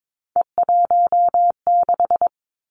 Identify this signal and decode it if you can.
Morse code, heard as a keyed tone, about 22 wpm, 700 Hz. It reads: E16
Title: Three Challenging Letter-Number Combinations 22wpm